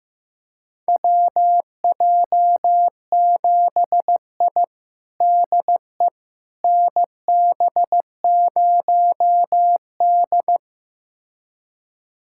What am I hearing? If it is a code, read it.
Morse code, heard as a keyed tone, about 15 wpm, 700 Hz. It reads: WJ7I DE NB0D
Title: USA Callsign DE Callsign - 4 Characters - 15wpm